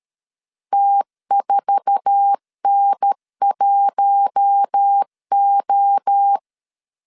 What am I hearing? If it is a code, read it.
Morse code, heard as a keyed tone, about 13 wpm, 800 Hz. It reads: T4N1O
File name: LEZIONI / 8 YJ45 / 8 YJ4 12wpm 2151